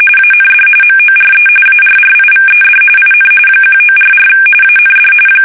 RTTY, about 45.45 baud. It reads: AND MORE ARRIVED IN 1944 AND
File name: fsk850